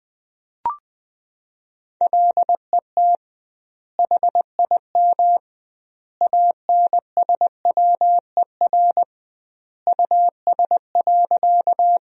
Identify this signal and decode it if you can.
Morse code, heard as a keyed tone, about 20 wpm, 700 Hz. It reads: LET HIM ANSWER US.
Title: 2x Word-Spacing - Sentences from Top 300 Words - 20wpm